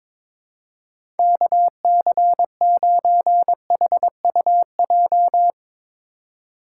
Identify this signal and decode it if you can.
Morse code, heard as a keyed tone, about 22 wpm, 700 Hz. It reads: KC9HUJ
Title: USA Call Signs - 4x Character Spacing - 22wpm